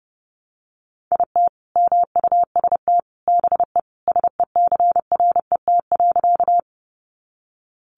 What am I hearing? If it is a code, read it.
Morse code, heard as a keyed tone, about 30 wpm, 700 Hz. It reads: IT MUST BE SECRET.
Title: Sentences from Top 1200 Words 30wpm